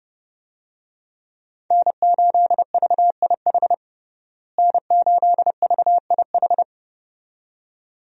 Morse code, heard as a keyed tone, about 30 wpm, 700 Hz. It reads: N8VIH N8VIH
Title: USA Call Signs - 5 Characters - Repeated 2x - 30wpm